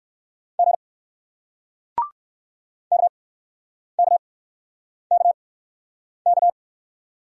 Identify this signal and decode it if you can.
Morse code, heard as a keyed tone, about 60 wpm, spaced as for 9 wpm, 700 Hz. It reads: G KKKK